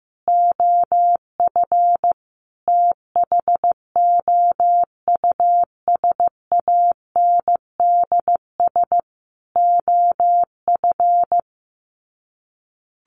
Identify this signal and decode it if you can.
Morse code, heard as a keyed tone, about 15 wpm, 700 Hz. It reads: OF THOUSANDS OF
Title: Sets of 4 Words 15wpm